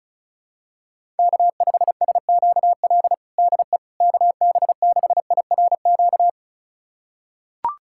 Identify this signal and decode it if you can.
Morse code, heard as a keyed tone, about 36 wpm, 700 Hz. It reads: K5SQL DE KB6IRQ
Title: USA Callsign DE Callsign 35wpm